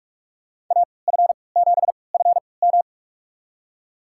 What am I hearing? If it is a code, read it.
Morse code, heard as a keyed tone, about 42 wpm, spaced as for 25 wpm, 700 Hz. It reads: AF7FM
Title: USA Call Signs - 5 Characters - 3x Character Spacing - 45wpm